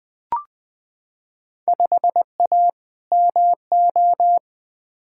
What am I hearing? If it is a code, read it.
Morse code, heard as a keyed tone, about 20 wpm, 700 Hz. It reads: 5A MO
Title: Field Day Exchanges Encore - 20wpm